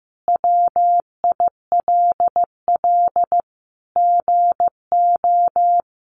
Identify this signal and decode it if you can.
Morse code, heard as a keyed tone, about 15 wpm, 700 Hz. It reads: WILL GO
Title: Sentences from Top 100 Words 15wpm